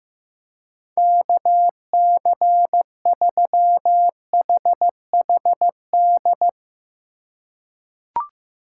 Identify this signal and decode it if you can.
Morse code, heard as a keyed tone, about 15 wpm, 700 Hz. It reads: KC3HHD
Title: USA Call Signs - 4x Character Spacing - 15wpm